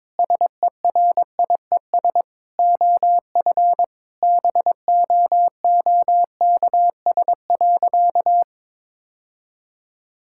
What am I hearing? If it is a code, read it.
Morse code, heard as a keyed tone, about 21 wpm, 700 Hz. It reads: SERIES OF BOOKS.
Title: Sentences from Top 1000 Words - Review Part 5 - 22wpm